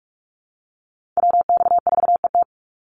Morse code, heard as a keyed tone, about 44 wpm, 700 Hz. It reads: WX4ET
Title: NA SOTA Chaser Callsigns 45wpm